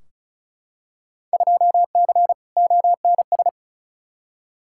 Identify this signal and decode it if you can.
Morse code, heard as a keyed tone, about 36 wpm, 700 Hz. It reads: 2C ONS
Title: Field Day Exchanges Encore - 35wpm